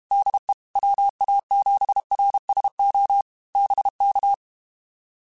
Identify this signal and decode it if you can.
Morse code, heard as a keyed tone, about 32 wpm, 800 Hz. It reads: DE WA7RSO BK